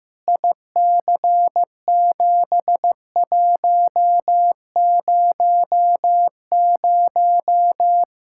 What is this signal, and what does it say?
Morse code, heard as a keyed tone, about 16 wpm, 700 Hz. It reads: IC7100
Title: Practice QSO - Exchange 2 - Info - 15wpm